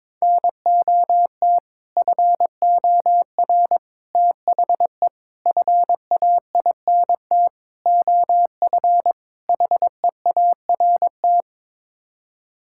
Morse code, heard as a keyed tone, about 21 wpm, 700 Hz. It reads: NOT FOR THE FAINT OF HEART
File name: practice / 22wpm / Sets of 6 Words - 22wpm